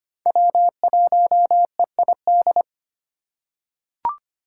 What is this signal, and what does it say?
Morse code, heard as a keyed tone, about 25 wpm, 700 Hz. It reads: W1EID
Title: USA Call Signs - 2x Character Spacing - 25wpm